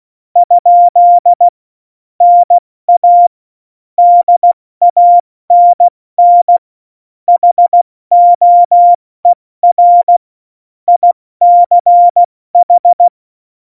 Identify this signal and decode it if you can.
Morse code, heard as a keyed tone, about 16 wpm, 700 Hz. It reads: ? NA DANN HOER ICH
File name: Morsekurs / Witze / Witz 12. Juni 2018, 70 F80